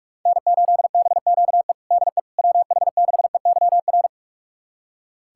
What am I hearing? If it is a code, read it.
Morse code, heard as a keyed tone, about 46 wpm, 700 Hz. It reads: N8BQE DE WH6EYR
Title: USA Callsign DE Callsign 45wpm